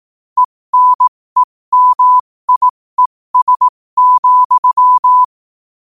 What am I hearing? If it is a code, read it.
Morse code, heard as a keyed tone, about 18 wpm, spaced as for 15 wpm, 1000 Hz. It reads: ENEMIES,
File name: qotdfiles / 15wpm